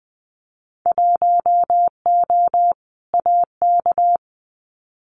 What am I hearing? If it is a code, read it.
Morse code, heard as a keyed tone, about 20 wpm, 700 Hz. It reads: 1O AK
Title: Winter Field Day Exchanges Encore 20wpm